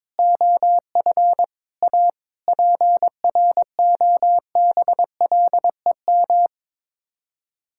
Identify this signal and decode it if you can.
Morse code, heard as a keyed tone, about 22 wpm, 700 Hz. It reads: OF A PROBLEM
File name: practice / 22wpm / Sets of 7 Words - 22wpm